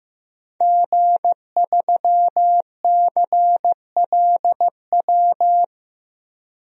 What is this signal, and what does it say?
Morse code, heard as a keyed tone, about 15 wpm, 700 Hz. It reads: G3CLW
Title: DXCC-100 Call Signs - 5 Characters - Repeated 3x - 15wpm